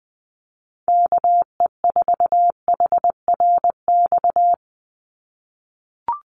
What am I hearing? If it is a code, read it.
Morse code, heard as a keyed tone, about 20 wpm, 700 Hz. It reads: KE4HRX
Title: USA Call Signs 20wpm